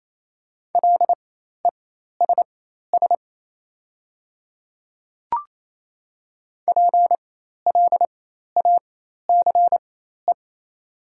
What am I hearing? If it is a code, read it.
Morse code, heard as a keyed tone, about 28 wpm, spaced as for 13 wpm, 690 Hz. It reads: LESS PLACE
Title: 4x Character Spacing - Top 500 Words - 28wpm